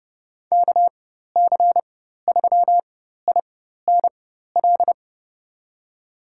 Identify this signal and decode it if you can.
Morse code, heard as a keyed tone, about 30 wpm, spaced as for 14 wpm, 700 Hz. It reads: KC3INL
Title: USA Call Signs - 6 Characters - 4x Character Spacing - 30wpm